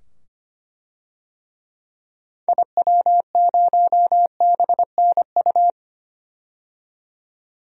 Morse code, heard as a keyed tone, about 25 wpm, 700 Hz. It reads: IW0BNU
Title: DXCC-100 Call Signs - 6 Characters - 25wpm